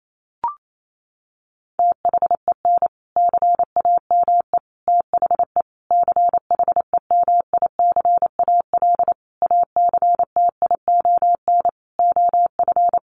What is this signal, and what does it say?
Morse code, heard as a keyed tone, about 28 wpm, 700 Hz. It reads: THEN CAME THE CHEMICAL ACTION OF
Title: Sentences from Top 1200 Words 28wpm